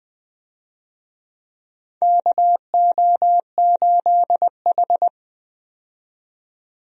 Morse code, heard as a keyed tone, about 21 wpm, 700 Hz. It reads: KO8H
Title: USA Call Signs - 4 Characters - 20wpm